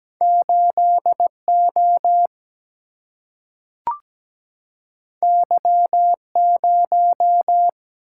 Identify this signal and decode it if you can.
Morse code, heard as a keyed tone, about 17 wpm, 700 Hz. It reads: 8O Y0
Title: Single-Letter Single-Number 17wpm